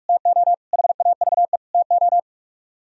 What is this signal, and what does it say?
Morse code, heard as a keyed tone, about 67 wpm, spaced as for 55 wpm, 700 Hz. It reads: TO HAVE TO
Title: Speed-Racing - Sets of 4 Words 45wpm